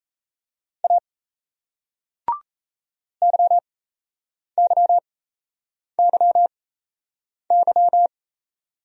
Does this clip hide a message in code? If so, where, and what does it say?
Yes: Morse code from 0.8 to 8.1 s, heard as a keyed tone, about 38 wpm, spaced as for 8 wpm, 700 Hz. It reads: A YYYY